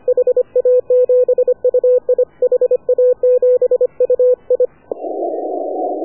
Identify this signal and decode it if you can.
Morse code, heard as a keyed tone, about 25 wpm, 500 Hz. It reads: HA7UI HA7UI
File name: AUDIO2019 / HA7UI CW